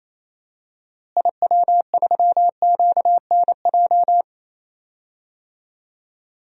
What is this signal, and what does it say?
Morse code, heard as a keyed tone, about 28 wpm, 700 Hz. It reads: IW3QNJ
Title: DXCC-100 Call Signs - 6 Characters - Repeated 3x - 28wpm